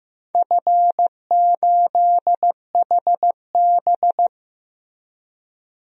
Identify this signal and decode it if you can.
Morse code, heard as a keyed tone, about 15 wpm, 700 Hz. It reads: F8HB
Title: DXCC-100 Call Signs - 4 Characters - 15wpm